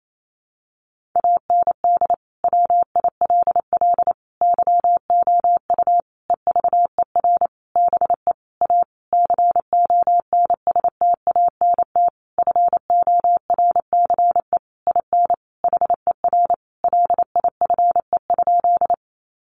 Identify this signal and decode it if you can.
Morse code, heard as a keyed tone, about 28 wpm, 700 Hz. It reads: AND WILL YOU EVER BE A CONSTANT FORCE IN HER LIFE?
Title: Sentences from Top 1500 Words 28wpm